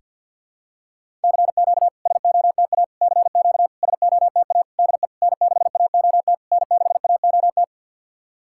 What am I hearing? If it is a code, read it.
Morse code, heard as a keyed tone, about 50 wpm, 700 Hz. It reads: CQ SOTA CQ SOTA DE N6AOT N6AOT
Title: SOTA QSO Practice 50wpm